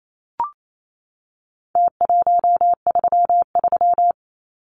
Morse code, heard as a keyed tone, about 28 wpm, 700 Hz. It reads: T133
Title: Numbers - Contest Style - 4-Digit 28wpm